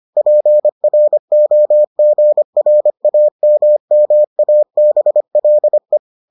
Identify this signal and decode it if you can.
Morse code, heard as a keyed tone, about 25 wpm, 600 Hz. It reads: PROGRAMMABLE